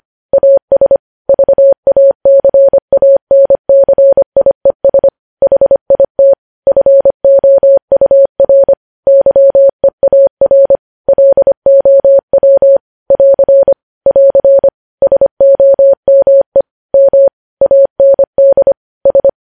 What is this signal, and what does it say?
Morse code, heard as a keyed tone, about 25 wpm, 550 Hz. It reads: AS VACANCIES HIT FOUR YEAR LOW <AR> <AR> SOME M AND S
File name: podcast / News headlines-1x-25x25